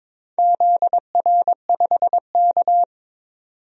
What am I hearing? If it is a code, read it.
Morse code, heard as a keyed tone, about 22 wpm, 700 Hz. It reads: ZR5K
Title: DXCC-100 Call Signs - 4 Characters - 22wpm